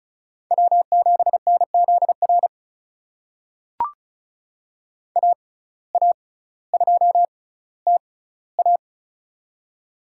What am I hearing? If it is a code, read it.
Morse code, heard as a keyed tone, about 36 wpm, spaced as for 18 wpm, 700 Hz. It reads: W7NZR AA2TA
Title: USA Call Signs - 5 Characters - 6x Character Spacing - 35wpm